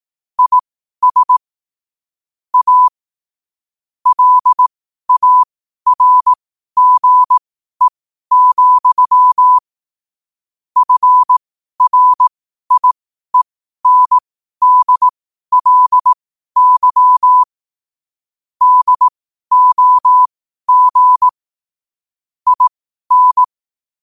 Morse code, heard as a keyed tone, about 18 wpm, spaced as for 12 wpm, 1000 Hz. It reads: IS A LARGE, FRIENDLY DOG IN
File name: qotdfiles / 12wpm